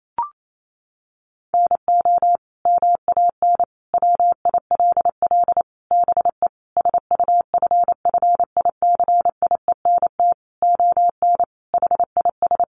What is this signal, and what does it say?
Morse code, heard as a keyed tone, about 28 wpm, 700 Hz. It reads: NO MAN WILL BE SUFFICIENT ON HIS
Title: Sentences from Top 1400 Words 28wpm